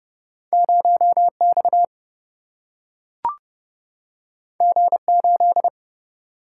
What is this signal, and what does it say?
Morse code, heard as a keyed tone, about 30 wpm, 700 Hz. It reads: 0X G8
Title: Single-Letter Single-Number 30wpm